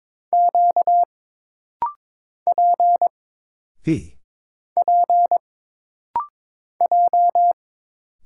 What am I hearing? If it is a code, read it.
Morse code, heard as a keyed tone, about 22 wpm, 700 Hz. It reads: Q P P J
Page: This rapid-fire timing is geared to push your ICR (Instant Character Recognition) skills.